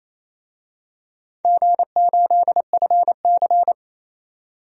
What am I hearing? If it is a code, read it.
Morse code, heard as a keyed tone, about 27 wpm, 690 Hz. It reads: G8FC
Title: DXCC-100 Call Signs - 4 Characters - 28wpm